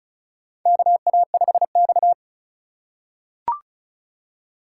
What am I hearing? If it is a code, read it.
Morse code, heard as a keyed tone, about 36 wpm, 700 Hz. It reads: KA5X
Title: USA Call Signs - 4 Characters - 4x Character Spacing - 35wpm